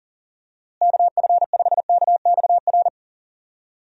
Morse code, heard as a keyed tone, about 40 wpm, 700 Hz. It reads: KF5KXR
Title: USA Call Signs - 6 Characters - Repeated 3x - 40wpm